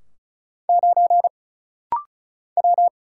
Morse code, heard as a keyed tone, about 36 wpm, 700 Hz. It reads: 9 W
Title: Single Letter-Number - Rapid-Fire 35wpm